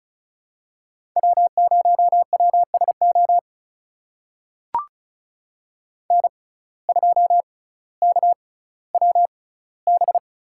Morse code, heard as a keyed tone, about 36 wpm, spaced as for 18 wpm, 700 Hz. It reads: W0WSO N2KWB
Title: USA Call Signs - 5 Characters - 6x Character Spacing - 35wpm